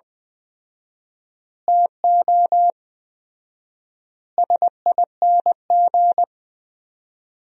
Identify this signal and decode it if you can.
Morse code, heard as a keyed tone, about 20 wpm, 700 Hz. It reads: TO SING
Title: 4x Word-Spacing - Sentences from Top 400 Words - 20wpm